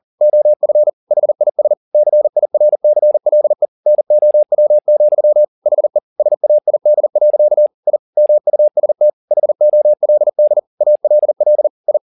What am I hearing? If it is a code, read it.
Morse code, heard as a keyed tone, about 40 wpm, 600 Hz. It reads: OF HIS CIRCLE NOW, HE SAID. I MUST HOLD ALL I
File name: the old man and the sea 2 40wpm